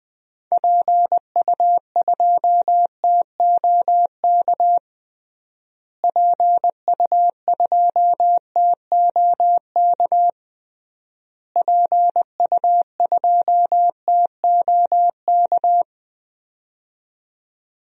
Morse code, heard as a keyed tone, about 21 wpm, 700 Hz. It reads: PU2TOK PU2TOK PU2TOK
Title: DXCC-100 Call Signs - 6 Characters - Repeated 3x - 20wpm